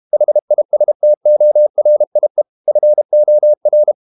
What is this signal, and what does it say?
Morse code, heard as a keyed tone, about 32 wpm, 600 Hz. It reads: HISTORIE FOR
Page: Ferie 32 wpm